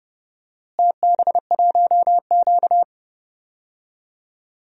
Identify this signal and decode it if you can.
Morse code, heard as a keyed tone, about 31 wpm, 700 Hz. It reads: TB1Q
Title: DXCC-100 Call Signs - 4 Characters - Repeated 2x - 30wpm